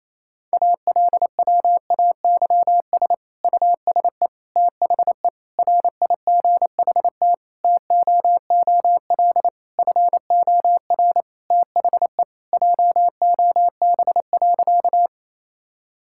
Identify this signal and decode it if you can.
Morse code, heard as a keyed tone, about 28 wpm, 700 Hz. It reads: ALWAYS USE THE RIGHT TOOL FOR THE JOB.
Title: Sentences from Top 1000 Words - Review Part 6 - 28wpm